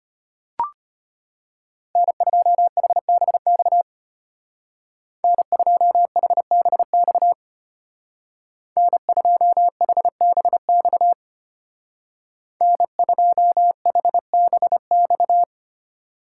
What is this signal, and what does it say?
Morse code, heard as a keyed tone, about 33 wpm, 700 Hz. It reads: N2HBX N2HBX N2HBX N2HBX